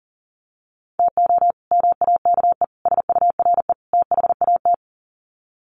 Morse code, heard as a keyed tone, about 40 wpm, 700 Hz. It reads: TO MAKE SURE THAT
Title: Sets of 4 Words 40wpm